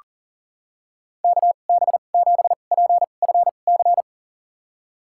Morse code, heard as a keyed tone, about 40 wpm, spaced as for 29 wpm, 700 Hz. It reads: KB7PFC
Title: USA Call Signs - 6 Characters - 2x Character Spacing - 40wpm